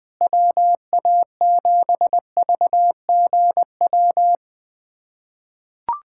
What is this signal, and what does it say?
Morse code, heard as a keyed tone, about 21 wpm, 690 Hz. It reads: WA7VGW
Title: SOTA QSO Practice 20wpm